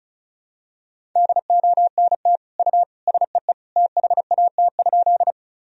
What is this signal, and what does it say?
Morse code, heard as a keyed tone, about 36 wpm, 700 Hz. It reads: DONT U SEE THAT?
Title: Sentences from Top 300 Words and Abbreviations 35wpm